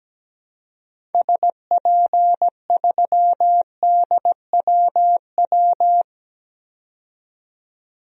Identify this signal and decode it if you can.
Morse code, heard as a keyed tone, about 17 wpm, 700 Hz. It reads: SP3DWW
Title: DXCC-100 Call Signs - 6 Characters - 17wpm